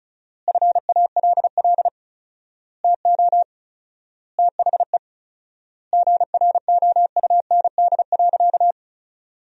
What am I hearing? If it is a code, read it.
Morse code, heard as a keyed tone, about 34 wpm, 700 Hz. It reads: FALL TO THE GROUND.
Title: 4x Word-Spacing - Sentences from Top 500 Words - 35wpm